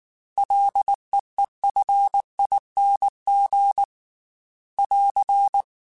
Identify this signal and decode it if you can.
Morse code, heard as a keyed tone, about 19 wpm, 800 Hz. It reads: LEEFING <AR>
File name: Ls / Single L m3s / Single L Prefixe 3x2 Twice @18